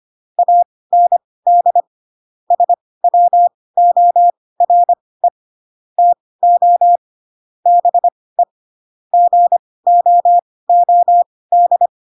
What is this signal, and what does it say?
Morse code, heard as a keyed tone, about 25 wpm, spaced as for 18 wpm, 700 Hz. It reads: AND SWORE TO BE GOOD
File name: Pinocchio / Pinocchio ch2 12 25 ik1hge